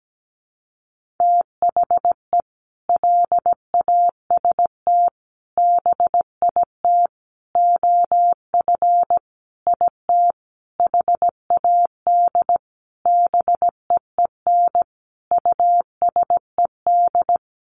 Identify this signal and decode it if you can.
Morse code, heard as a keyed tone, about 17 wpm, 700 Hz. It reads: THE LAST BIT OF IT HAD BEEN USED
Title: Sentences from Top 900 Words 17wpm